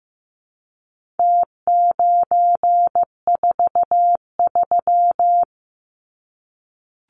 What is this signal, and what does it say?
Morse code, heard as a keyed tone, about 15 wpm, 700 Hz. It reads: T943
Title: Numbers - Contest Style - 4-Digit 15wpm